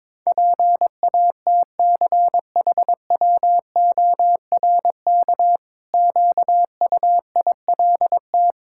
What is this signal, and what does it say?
Morse code, heard as a keyed tone, about 22 wpm, 700 Hz. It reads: PATCHWORK QUILT
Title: Pangram Challenge v2 - 22wpm